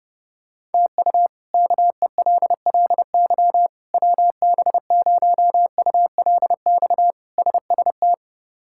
Morse code, heard as a keyed tone, about 30 wpm, 700 Hz. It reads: TU KELLY WB0ULX SST
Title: K1USN SST QSO Practice - 30wpm